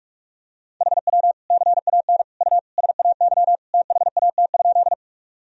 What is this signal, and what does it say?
Morse code, heard as a keyed tone, about 45 wpm, 700 Hz. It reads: HW CAN U SAY THAT?
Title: Sentences from Top 200 Words and Abbreviations 45wpm